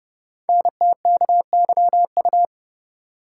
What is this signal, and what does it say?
Morse code, heard as a keyed tone, about 30 wpm, 700 Hz. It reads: NTKYU
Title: Code Groups - 5 Letters - 30wpm